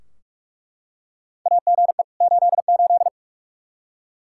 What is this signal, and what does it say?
Morse code, heard as a keyed tone, about 41 wpm, 700 Hz. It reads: AGE 88
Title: Practice QSO - Exchange 3 - More Info - 45wpm